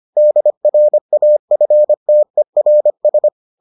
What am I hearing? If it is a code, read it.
Morse code, heard as a keyed tone, about 25 wpm, 600 Hz. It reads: DRAFTERS